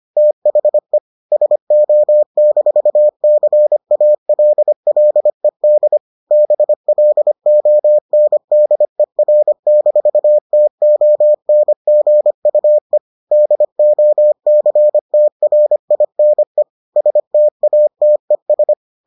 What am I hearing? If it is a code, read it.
Morse code, heard as a keyed tone, about 25 wpm, 600 Hz. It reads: THE SO-CALLED BLONDER-TONGUE DOCTRINE STATES